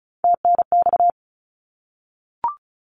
Morse code, heard as a keyed tone, about 36 wpm, 700 Hz. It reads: TNX
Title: Abbreviations 35wpm